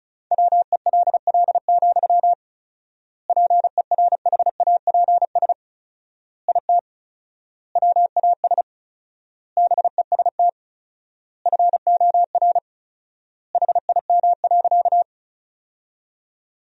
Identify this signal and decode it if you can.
Morse code, heard as a keyed tone, about 34 wpm, 700 Hz. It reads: WELL, PERHAPS IT WAS BEST FOR HIM.
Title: 4x Word-Spacing - Sentences from Top 500 Words - 35wpm